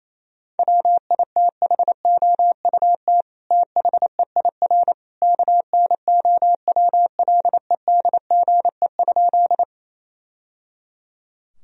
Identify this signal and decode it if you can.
Morse code, heard as a keyed tone, about 28 wpm, 700 Hz. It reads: WITHOUT THEIR KNOWLEDGE?
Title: Sentences from Top 1000 Words - Review Part 3 - 28wpm